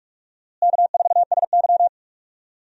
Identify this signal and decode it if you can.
Morse code, heard as a keyed tone, about 46 wpm, 700 Hz. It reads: K4SY
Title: USA Call Signs - 4 Characters - Repeated 2x - 45wpm